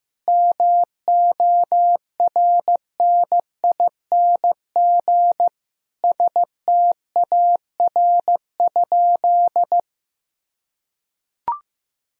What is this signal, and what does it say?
Morse code, heard as a keyed tone, about 15 wpm, 700 Hz. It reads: MORNING STAR?
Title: Sentences from Top 1000 Words - Review Part 2 - 15wpm